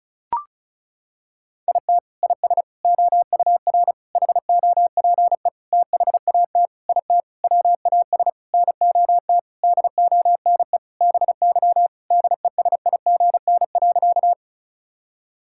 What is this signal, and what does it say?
Morse code, heard as a keyed tone, about 36 wpm, 700 Hz. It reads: IT IS OUR HOPE THAT IT WAS NOT DONE BY DESIGN.
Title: Sentences from Top 1000 Words - Review Part 5 - 35wpm